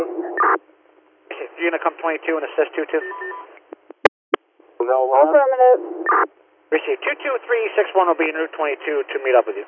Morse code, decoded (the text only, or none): none